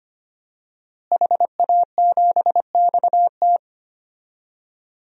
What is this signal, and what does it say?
Morse code, heard as a keyed tone, about 25 wpm, 700 Hz. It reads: HA7XT
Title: DXCC-100 Call Signs - 5 Characters - Repeated 2x - 25wpm